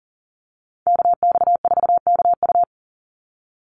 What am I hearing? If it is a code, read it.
Morse code, heard as a keyed tone, about 40 wpm, 700 Hz. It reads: KX4KU
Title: Call Signs - 40wpm